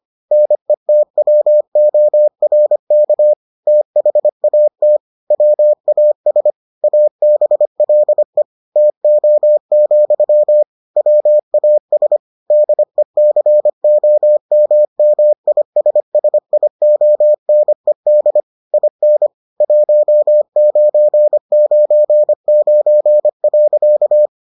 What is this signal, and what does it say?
Morse code, heard as a keyed tone, about 25 wpm, 600 Hz. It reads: NETWORK THAT WAS ABLE TO, WAS DECOMMISSIONED IN 1999.